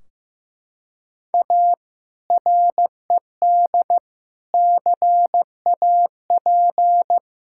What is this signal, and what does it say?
Morse code, heard as a keyed tone, about 16 wpm, 700 Hz. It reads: A RED CAP
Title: Sentences from Top 1600 Words 15wpm